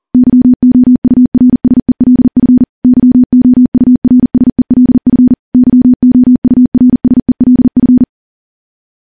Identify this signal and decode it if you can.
Morse code, heard as a keyed tone, about 40 wpm, 250 Hz. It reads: YOURSELF YOURSELF YOURSELF
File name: morse / morse-training-40WPM